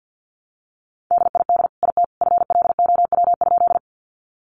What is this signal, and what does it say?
Morse code, heard as a keyed tone, about 50 wpm, 700 Hz. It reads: DID IT FLOW?
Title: Sentences from Top 1200 Words 50wpm